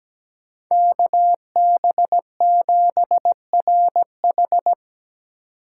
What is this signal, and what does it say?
Morse code, heard as a keyed tone, about 17 wpm, 700 Hz. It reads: KB7RH
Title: USA Call Signs - 5 Characters - 3x Character Spacing - 17wpm